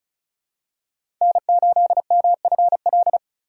Morse code, heard as a keyed tone, about 36 wpm, 700 Hz. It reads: N8MFL
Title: USA Call Signs - 5 Characters - 4x Character Spacing - 35wpm